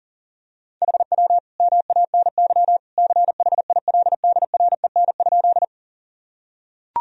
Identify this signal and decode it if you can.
Morse code, heard as a keyed tone, about 37 wpm, 700 Hz. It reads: HW MANY CHILDREN?
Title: Sentences from Top 300 Words and Abbreviations 40wpm